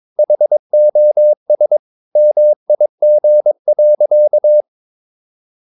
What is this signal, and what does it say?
Morse code, heard as a keyed tone, about 22 wpm, 600 Hz. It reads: HOS MIG.
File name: Ferie QRS 22wpm